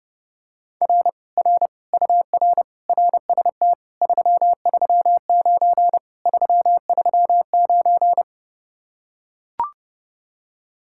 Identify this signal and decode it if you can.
Morse code, heard as a keyed tone, about 32 wpm, 700 Hz. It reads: R R UR RST 339 339